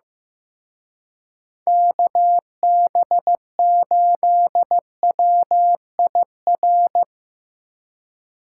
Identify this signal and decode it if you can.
Morse code, heard as a keyed tone, about 18 wpm, 700 Hz. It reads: KB8WIR
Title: Speed Racing - USA 2x3 Call Signs 15wpm